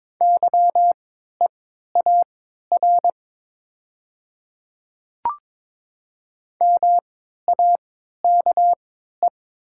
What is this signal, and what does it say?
Morse code, heard as a keyed tone, about 22 wpm, spaced as for 12 wpm, 700 Hz. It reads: YEAR MAKE
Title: 3x Character Spacing - Top 500 Words - 22wpm